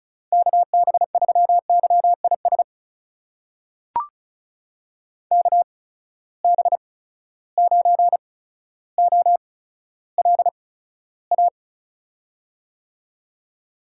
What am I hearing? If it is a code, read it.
Morse code, heard as a keyed tone, about 36 wpm, spaced as for 15 wpm, 700 Hz. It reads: KB3YIS KB9OLA
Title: USA Call Signs - 8x Character Spacing - 35wpm